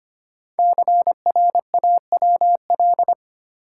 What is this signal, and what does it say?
Morse code, heard as a keyed tone, about 25 wpm, 700 Hz. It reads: CRAWL
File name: 5 Letter Words - First 3 Letters Given - 25wpm